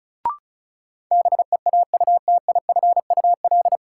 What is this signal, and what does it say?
Morse code, heard as a keyed tone, about 34 wpm, 700 Hz. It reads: BEAUTIFUL
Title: Sentences from Top 900 Words - New Word Drill - 35wpm